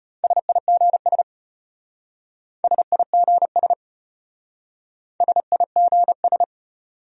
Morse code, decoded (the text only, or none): SIGS SIGS SIGS